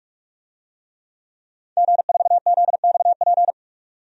Morse code, heard as a keyed tone, about 46 wpm, 700 Hz. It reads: G4ZXP
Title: DXCC-100 Call Signs - 5 Characters - 45wpm